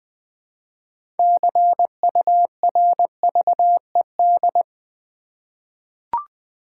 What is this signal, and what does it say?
Morse code, heard as a keyed tone, about 20 wpm, 700 Hz. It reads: CURVED
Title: 6 Letter Words - First 3 Letters Given - 20wpm